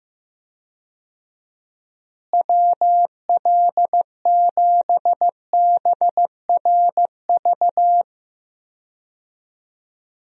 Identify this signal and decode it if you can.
Morse code, heard as a keyed tone, about 16 wpm, 700 Hz. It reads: WL7BRV
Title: USA Call Signs - 6 Characters - Repeated 2x - 15wpm